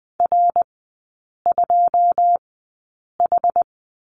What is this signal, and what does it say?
Morse code, heard as a keyed tone, about 20 wpm, spaced as for 8 wpm, 700 Hz. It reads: R2H
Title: 081 Lesson 14e - Number 2 - Three Characters 20wpm